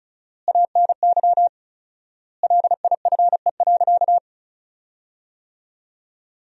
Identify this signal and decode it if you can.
Morse code, heard as a keyed tone, about 36 wpm, 700 Hz. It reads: ANY LIFE.
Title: 4x Word-Spacing - Sentences from Top 300 Words - 35wpm